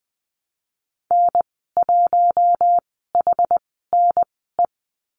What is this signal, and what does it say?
Morse code, heard as a keyed tone, about 20 wpm, spaced as for 14 wpm, 700 Hz. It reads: N1HNE
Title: 062 Lesson 11g - Letter h - Call Signs 20wpm